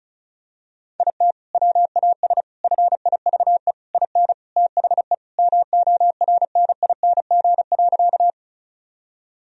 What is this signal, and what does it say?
Morse code, heard as a keyed tone, about 34 wpm, 700 Hz. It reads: IT WAS FIVE IN THE MORNING.
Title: Sentences from Top 1000 Words - Review Part 2 - 35wpm